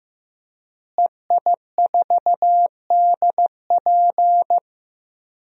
Morse code, heard as a keyed tone, about 15 wpm, 700 Hz. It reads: EI4DP
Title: DXCC-100 Call Signs - 5 Characters - Repeated 2x - 15wpm